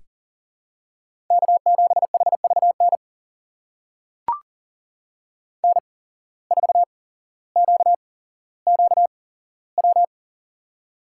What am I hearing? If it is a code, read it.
Morse code, heard as a keyed tone, about 38 wpm, spaced as for 17 wpm, 700 Hz. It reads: K7HVN N4QQW
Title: USA Call Signs - 5 Characters - 8x Character Spacing - 40wpm